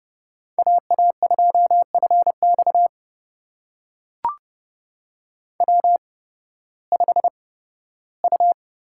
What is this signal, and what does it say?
Morse code, heard as a keyed tone, about 29 wpm, spaced as for 16 wpm, 700 Hz. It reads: AA2FX W5U
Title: USA Call Signs - 5 Characters - 8x Character Spacing - 30wpm